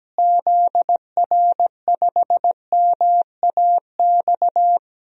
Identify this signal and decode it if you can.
Morse code, heard as a keyed tone, about 17 wpm, 700 Hz. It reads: ZR5MAX
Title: DXCC-100 Call Signs - 6 Characters - Repeated 3x - 17wpm